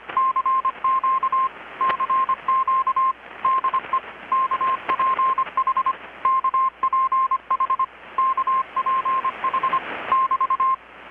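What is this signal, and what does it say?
Morse code, heard as a keyed tone, about 25 wpm, 1060 Hz, with radio noise. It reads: CQ CQ DE KPH KPH KPH <BT>
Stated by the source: Morse from KPH's Night of Nights XX maritime radiotelegraphy event, received in Chiapas, Mexico about 3,800 km away on a Kenwood R-600 with a grounded 10 m long-wire antenna.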